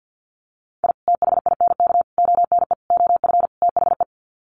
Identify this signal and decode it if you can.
Morse code, heard as a keyed tone, about 48 wpm, 700 Hz. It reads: I THINK ONE OF THE
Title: Sets of 5 Words 50wpm